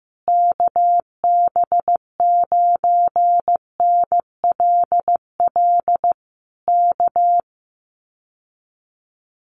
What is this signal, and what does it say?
Morse code, heard as a keyed tone, about 15 wpm, 700 Hz. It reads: KB9NLL K
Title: Common QSO Phrases Encore 15wpm